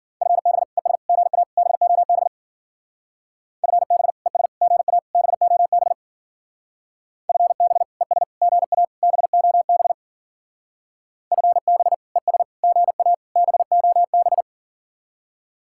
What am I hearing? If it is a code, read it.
Morse code, heard as a keyed tone, about 49 wpm, 700 Hz. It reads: FB ES GA BOB FB ES GA BOB FB ES GA BOB FB ES GA BOB
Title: Speed-Racing Common QSO Phrases - 40wpm